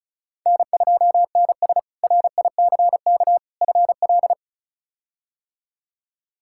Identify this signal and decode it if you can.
Morse code, heard as a keyed tone, about 36 wpm, 700 Hz. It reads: N2NS RICK FL
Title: K1USN SST QSO Practice - 35wpm